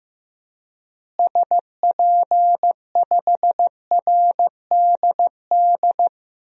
Morse code, heard as a keyed tone, about 15 wpm, 700 Hz. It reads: SP5RDD
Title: DXCC-100 Call Signs - 6 Characters - Repeated 2x - 15wpm